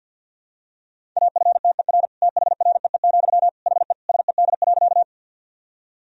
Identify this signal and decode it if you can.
Morse code, heard as a keyed tone, about 50 wpm, 700 Hz. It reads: AFTER THREE, HE SED.
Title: Sentences from Top 200 Words and Abbreviations 50wpm